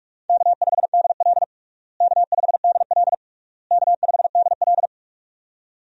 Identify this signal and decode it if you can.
Morse code, heard as a keyed tone, about 45 wpm, 700 Hz. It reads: K5DL K5DL K5DL
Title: USA Call Signs - 4 Characters - Repeated 3x - 45wpm